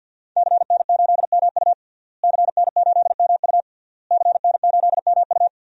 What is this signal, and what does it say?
Morse code, heard as a keyed tone, about 51 wpm, 700 Hz. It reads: CN8MU CN8MU CN8MU
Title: DXCC-100 Call Signs - 5 Characters - Repeated 3x - 50wpm